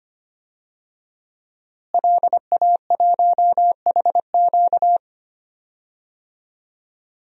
Morse code, heard as a keyed tone, about 25 wpm, 700 Hz. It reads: LA1HQ
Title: DXCC-100 Call Signs - 5 Characters - Repeated 3x - 25wpm